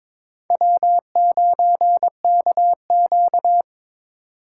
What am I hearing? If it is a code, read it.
Morse code, heard as a keyed tone, about 22 wpm, 700 Hz. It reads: W9KQ
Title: USA Call Signs - 4 Characters - 4x Character Spacing - 22wpm